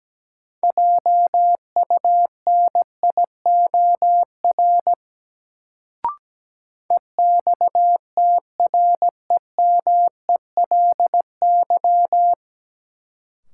Morse code, heard as a keyed tone, about 17 wpm, 700 Hz. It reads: JUNIOR EXTREMELY
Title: Sentences from Top 1000 Words - Review Part 7 - New Word Drill - 17wpm